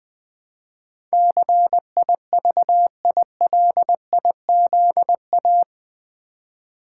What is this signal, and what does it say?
Morse code, heard as a keyed tone, about 20 wpm, 700 Hz. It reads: CIVILIZA
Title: Build A Word - 4 Letters At A Time - 12-Letter Words - 20wpm